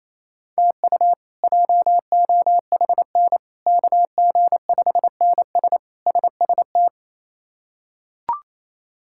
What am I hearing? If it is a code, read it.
Morse code, heard as a keyed tone, about 28 wpm, 700 Hz. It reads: TU JOHN KG5NS SST
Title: K1USN SST QSO Practice - 28wpm